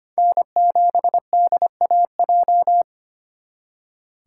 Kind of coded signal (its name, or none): Morse code